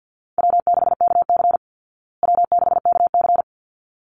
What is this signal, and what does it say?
Morse code, heard as a keyed tone, about 50 wpm, 700 Hz. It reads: W6KC W6KC
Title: Call Signs Repeated 3x - 50wpm